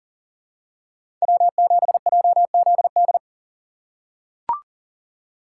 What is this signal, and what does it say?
Morse code, heard as a keyed tone, about 40 wpm, 700 Hz. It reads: W7JZD
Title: USA Call Signs - 5 Characters - 4x Character Spacing - 40wpm